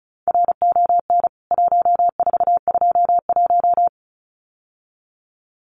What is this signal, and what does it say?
Morse code, heard as a keyed tone, about 34 wpm, 700 Hz. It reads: RON 1421
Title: CWT Exchange 35wpm